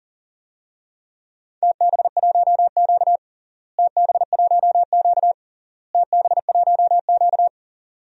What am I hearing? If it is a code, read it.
Morse code, heard as a keyed tone, about 40 wpm, 700 Hz. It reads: TB1Q TB1Q TB1Q
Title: DXCC-100 Call Signs - 4 Characters - Repeated 3x - 40wpm